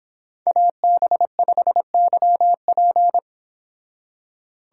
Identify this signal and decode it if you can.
Morse code, heard as a keyed tone, about 26 wpm, 700 Hz. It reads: AB5YP